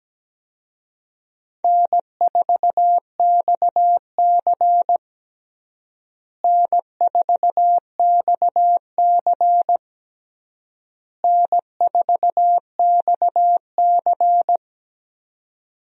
Morse code, heard as a keyed tone, about 17 wpm, 700 Hz. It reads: N4XC N4XC N4XC
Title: USA Call Signs - 4 Characters - Repeated 3x - 17wpm